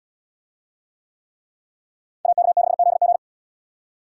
Morse code, heard as a keyed tone, about 72 wpm, 700 Hz. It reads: AK6ZG